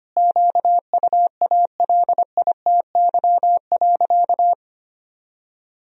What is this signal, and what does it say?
Morse code, heard as a keyed tone, about 25 wpm, 700 Hz. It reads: QUALITY.
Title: Sentences from Top 1000 Words - Review Part 4 - 25wpm